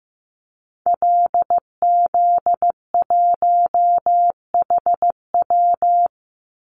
Morse code, heard as a keyed tone, about 15 wpm, 700 Hz. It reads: LZ1HW
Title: Call Signs Repeated 2x - 15wpm